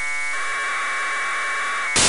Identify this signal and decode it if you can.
Packet radio (AX.25, as in APRS).